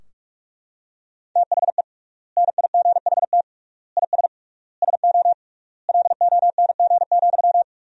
Morse code, heard as a keyed tone, about 42 wpm, 700 Hz. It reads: THE NIGHT IS SO LONG,
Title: 3x Word-Spacing - Sentences from Top 500 Words - 45wpm